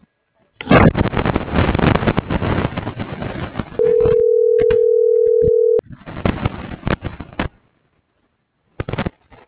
Ringback tone (North America), 440 + 480 Hz.